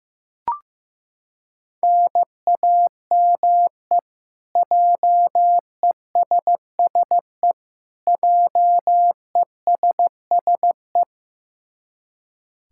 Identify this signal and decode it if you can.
Morse code, heard as a keyed tone, about 16 wpm, 700 Hz. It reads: NAME JESSE JESSE
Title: K1USN SST QSO Practice 15wpm